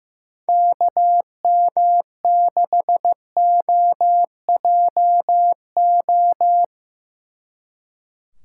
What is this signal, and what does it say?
Morse code, heard as a keyed tone, about 17 wpm, 700 Hz. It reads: KM6OJO
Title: Speed Racing - USA 2x3 Call Signs 15wpm